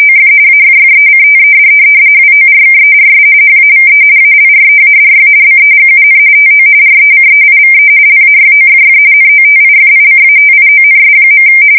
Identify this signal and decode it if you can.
RTTY, about 45.45 baud. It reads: OF AFFECTED WORKERS HAVE ALREADY BEGUN TO RECEIVE LAYOFF NOTIFICATIONS.
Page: 170 Shift 2125/2295